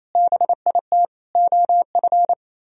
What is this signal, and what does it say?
Morse code, heard as a keyed tone, about 34 wpm, 700 Hz. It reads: BIT OF
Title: Speed-Racing - Sets of 4 Words 28wpm